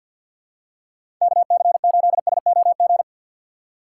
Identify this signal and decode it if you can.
Morse code, heard as a keyed tone, about 49 wpm, 700 Hz. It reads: KC8SOG
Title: USA Call Signs - 2x Character Spacing - 50wpm